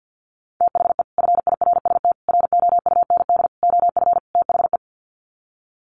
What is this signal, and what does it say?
Morse code, heard as a keyed tone, about 50 wpm, 700 Hz. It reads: THE FIRST ROUND OF THE
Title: Sets of 5 Words Encore 50wpm